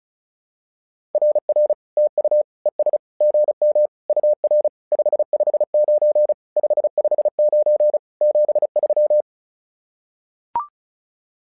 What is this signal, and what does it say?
Morse code, heard as a keyed tone, about 36 wpm, 600 Hz. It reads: RR TU ES GM UR 559 559 73
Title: SOTA QSO Practice 35wpm